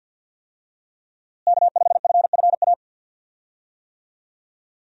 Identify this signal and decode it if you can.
Morse code, heard as a keyed tone, about 50 wpm, 700 Hz. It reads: K5FFA
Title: USA Call Signs - 5 Characters - Repeated 3x - 50wpm